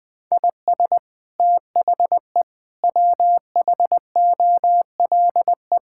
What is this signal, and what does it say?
Morse code, heard as a keyed tone, about 19 wpm, 700 Hz. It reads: IS THE WHOLE
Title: Sentences from Top 1000 Words - Review Part 5 - 20wpm